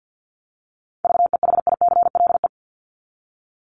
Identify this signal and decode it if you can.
Morse code, heard as a keyed tone, about 48 wpm, 700 Hz. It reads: VEHICLE
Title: Top 1751-2000 Words 50wpm